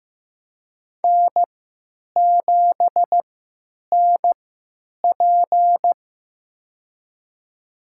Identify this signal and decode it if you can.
Morse code, heard as a keyed tone, about 15 wpm, spaced as for 9 wpm, 700 Hz. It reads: N7NP
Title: USA Call Signs - 3x Character Spacing - 15wpm